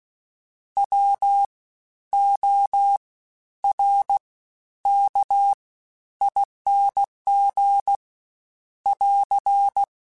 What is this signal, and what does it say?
Morse code, heard as a keyed tone, about 16 wpm, spaced as for 10 wpm, 800 Hz. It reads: WORKING <AR>
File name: Special INGs-mp3s / Special INGs-mp3s / SlowToFaster ing @15